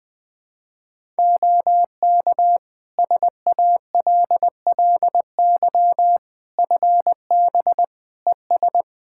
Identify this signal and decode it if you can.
Morse code, heard as a keyed tone, about 20 wpm, 700 Hz. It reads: OK SALLY FB ES
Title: Practice QSO - Exchange 2 - Info - 20wpm